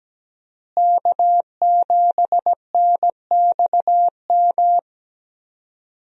Morse code, heard as a keyed tone, about 17 wpm, 700 Hz. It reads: K7NXM
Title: USA Call Signs - 5 Characters - Repeated 2x - 17wpm